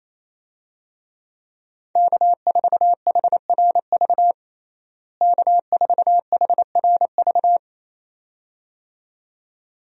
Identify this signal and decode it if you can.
Morse code, heard as a keyed tone, about 28 wpm, 700 Hz. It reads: K4HRV K4HRV
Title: USA Call Signs - 5 Characters - Repeated 2x - 28wpm